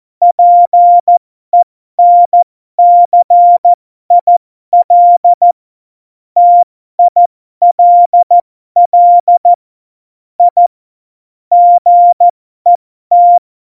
Morse code, heard as a keyed tone, about 14 wpm, spaced as for 12 wpm, 700 Hz. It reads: PENCIL TILL I GET
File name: Morsekurs / Witze